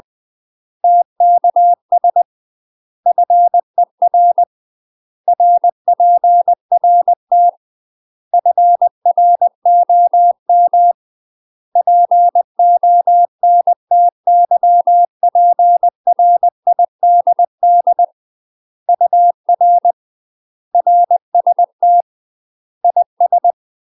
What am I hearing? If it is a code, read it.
Morse code, heard as a keyed tone, about 20 wpm, 700 Hz. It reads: TKS FER RPRT FROM PONTYPRIDD UR RST IS
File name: UK MASTER QSOs TRAINING / 20WPM / UK MASTER QSOs 0024